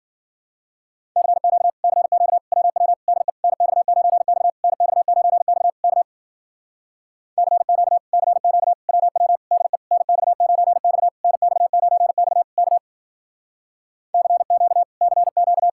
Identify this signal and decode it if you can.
Morse code, heard as a keyed tone, about 51 wpm, 700 Hz. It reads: CQ CQ WW DE NX9X NX9X K CQ CQ WW DE NX9X NX9X K CQ CQ